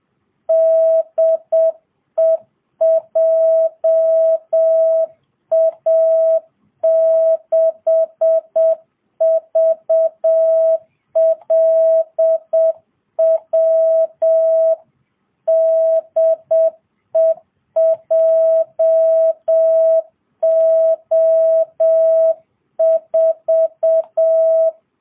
Morse code, heard as a keyed tone, about 7 wpm, 650 Hz. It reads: DE JA6VLW DE JO4